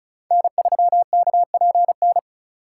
Morse code, heard as a keyed tone, about 36 wpm, 700 Hz. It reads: N3KPN
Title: USA Call Signs - 5 Characters - 2x Character Spacing - 35wpm